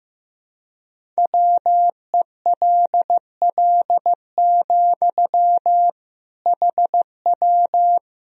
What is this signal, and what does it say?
Morse code, heard as a keyed tone, about 15 wpm, 700 Hz. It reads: WELL, HW
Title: Sentences from Top 200 Words and Abbreviations 15wpm